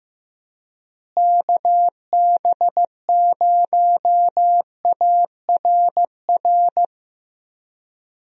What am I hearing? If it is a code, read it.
Morse code, heard as a keyed tone, about 15 wpm, 700 Hz. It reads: KB0ARR
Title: USA Call Signs - 6 Characters - Repeated 3x - 15wpm